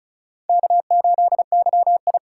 Morse code, heard as a keyed tone, about 36 wpm, 700 Hz. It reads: K8YI
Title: USA Call Signs - 4 Characters - 2x Character Spacing - 35wpm